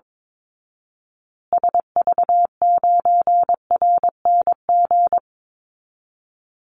Morse code, heard as a keyed tone, about 22 wpm, 700 Hz. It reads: SV9RNG
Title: Call Signs Repeated 3x - 22wpm